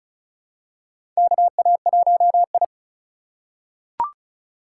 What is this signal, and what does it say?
Morse code, heard as a keyed tone, about 34 wpm, 700 Hz. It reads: KA1I
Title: USA Call Signs - 4 Characters - 3x Character Spacing - 35wpm